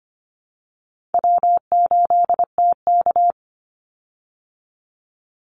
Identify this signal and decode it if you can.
Morse code, heard as a keyed tone, about 25 wpm, 700 Hz. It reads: W8TK
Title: Easy Call Signs Repeated 3x - 25wpm